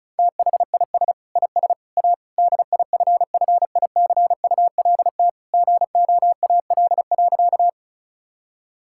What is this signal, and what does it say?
Morse code, heard as a keyed tone, about 36 wpm, 700 Hz. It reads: THIS IS A DIFFICULT GOAL.
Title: Sentences from Top 1000 Words - Review Part 5 - 35wpm